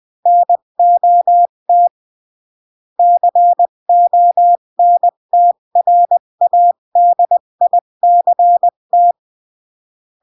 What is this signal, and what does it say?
Morse code, heard as a keyed tone, about 20 wpm, 700 Hz. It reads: NOT CONTRADICT
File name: CW / jodorowsky / 82 commandments 20-15